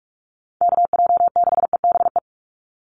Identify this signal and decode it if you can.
Morse code, heard as a keyed tone, about 44 wpm, 700 Hz. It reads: KJ6EBE
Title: USA Call Signs 45wpm